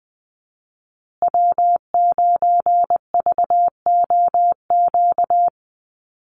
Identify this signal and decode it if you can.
Morse code, heard as a keyed tone, about 20 wpm, 700 Hz. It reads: W9VOQ
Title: USA Call Signs Repeated 3x 20wpm